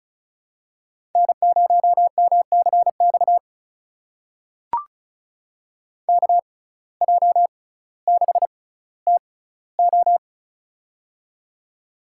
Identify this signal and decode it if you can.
Morse code, heard as a keyed tone, about 34 wpm, spaced as for 18 wpm, 700 Hz. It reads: N0MCX KJ6TO
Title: USA Call Signs - 5 Characters - 6x Character Spacing - 35wpm